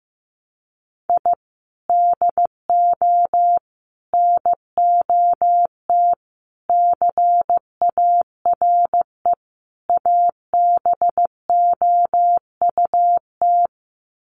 Morse code, heard as a keyed tone, about 15 wpm, 700 Hz. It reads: I DO NOT CARE ABOUT
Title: Sets of 5 Words Encore2 15wpm